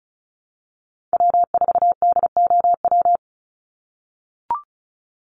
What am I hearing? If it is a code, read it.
Morse code, heard as a keyed tone, about 34 wpm, 700 Hz. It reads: W4DOW
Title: NA SOTA Chaser Callsigns 35wpm